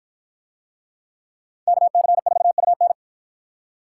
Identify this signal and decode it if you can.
Morse code, heard as a keyed tone, about 51 wpm, 700 Hz. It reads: KC4UN